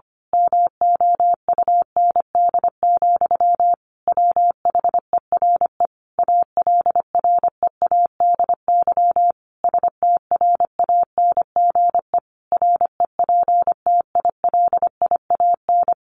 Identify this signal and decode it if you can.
Morse code, heard as a keyed tone, about 25 wpm, 700 Hz. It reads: MOUND, WHERE ALREADY STRANGE REPTILIAN
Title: A Sound Of Thunder 25wpm